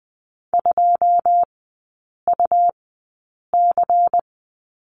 Morse code, heard as a keyed tone, about 19 wpm, 700 Hz. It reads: 2 U C
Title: 178 Lesson 29e - Slash - Three Characters 20wpm